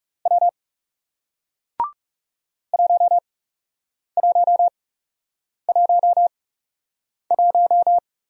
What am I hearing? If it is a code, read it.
Morse code, heard as a keyed tone, about 41 wpm, spaced as for 8 wpm, 700 Hz. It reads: W 1111